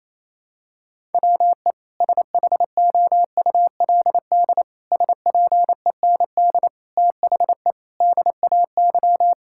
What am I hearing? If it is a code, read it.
Morse code, heard as a keyed tone, about 28 wpm, 700 Hz. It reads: WE SHOULD SPEND THE DAY
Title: Sentences from Top 1000 Words - Review Part 5 - 28wpm